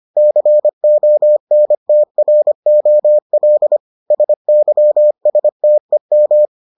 Morse code, heard as a keyed tone, about 25 wpm, 600 Hz. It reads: CONTROL SYSTEM